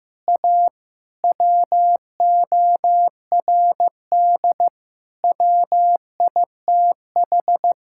Morse code, heard as a keyed tone, about 16 wpm, 700 Hz. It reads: A WORD WITH
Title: Sentences from Top 1000 Words - Review Part 6 - 15wpm